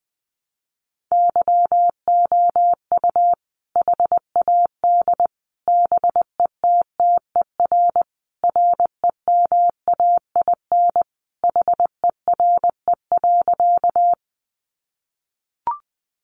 Morse code, heard as a keyed tone, about 20 wpm, 700 Hz. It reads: YOU HAD BETTER REMAIN HERE.
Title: Sentences from Top 1200 Words 20wpm